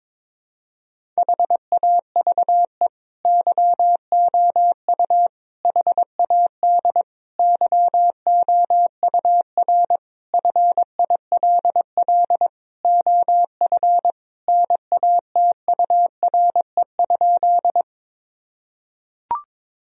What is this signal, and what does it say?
Morse code, heard as a keyed tone, about 21 wpm, 700 Hz. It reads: HAVE YOU HAD YOUR FILL OF NATURE?
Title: Sentences from Top 1000 Words - Review Part 3 - 22wpm